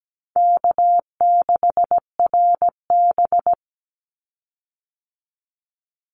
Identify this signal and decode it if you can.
Morse code, heard as a keyed tone, about 17 wpm, 700 Hz. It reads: K6RB
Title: Easy Call Signs Repeated 3x - 17wpm